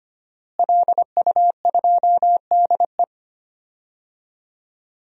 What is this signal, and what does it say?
Morse code, heard as a keyed tone, about 25 wpm, 700 Hz. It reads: LU2DE
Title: DXCC-100 Call Signs - 5 Characters - Repeated 3x - 25wpm